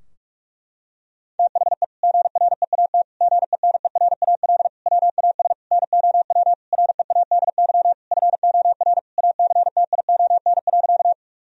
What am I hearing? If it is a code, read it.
Morse code, heard as a keyed tone, about 46 wpm, 700 Hz. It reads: THE GREAT GENERAL WAS NOW READY FOR ACTION.
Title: Sentences from Top 1000 Words - Review Part 4 - 45wpm